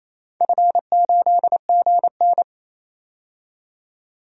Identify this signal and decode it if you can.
Morse code, heard as a keyed tone, about 28 wpm, 700 Hz. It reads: F8GN
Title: DXCC-100 Call Signs - 4 Characters - 28wpm